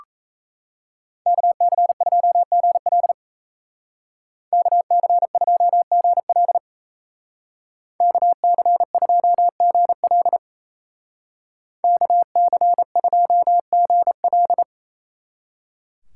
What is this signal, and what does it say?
Morse code, heard as a keyed tone, about 36 wpm, 700 Hz. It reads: KC2GL KC2GL KC2GL KC2GL